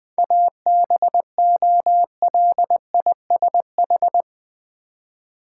Morse code, heard as a keyed tone, about 20 wpm, 690 Hz. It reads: ABOLISH
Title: 7 Letter Words - First 3 Letters Given - 20wpm